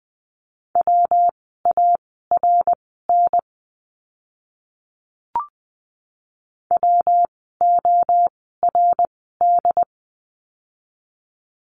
Morse code, heard as a keyed tone, about 20 wpm, spaced as for 15 wpm, 700 Hz. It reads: WARN WORD
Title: 115 Lesson 19f - Letter w - Words 20wpm